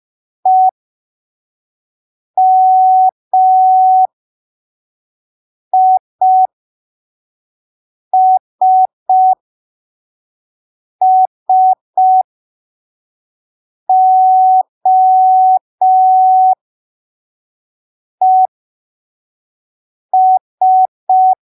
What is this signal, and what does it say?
Morse code, heard as a keyed tone, about 5 wpm, spaced as for 3 wpm, 750 Hz. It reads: EMISSOES
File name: arquivos / texto / texto 01 5ppm